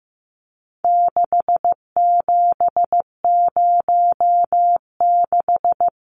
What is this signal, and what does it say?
Morse code, heard as a keyed tone, about 14 wpm, 700 Hz. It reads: 6706
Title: Numbers - 4-Digit 15wpm